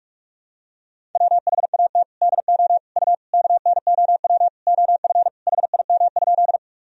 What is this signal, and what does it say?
Morse code, heard as a keyed tone, about 46 wpm, 700 Hz. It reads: WHAT DO U KNOW OF HIM?
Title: Sentences from Top 100 Words and Abbreviations 45wpm